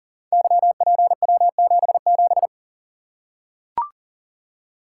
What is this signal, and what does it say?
Morse code, heard as a keyed tone, about 42 wpm, 700 Hz. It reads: YPW77
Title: Code Groups - 5 Letter:Numbers - 40wpm